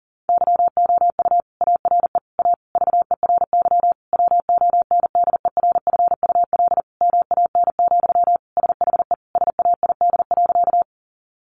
Morse code, heard as a keyed tone, about 40 wpm, 700 Hz. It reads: YOU ARE A VERY WONDERFUL MAN, SHE SAID.
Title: Sentences from Top 1100 Words 40wpm